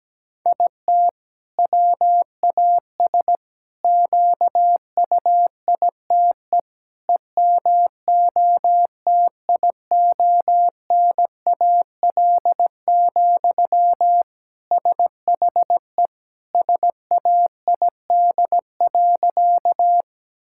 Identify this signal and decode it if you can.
Morse code, heard as a keyed tone, about 17 wpm, 700 Hz. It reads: IT WAS QUITE EMOTIONAL, SHE SAID.
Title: Sentences from Top 1000 Words - Review Part 5 - 17wpm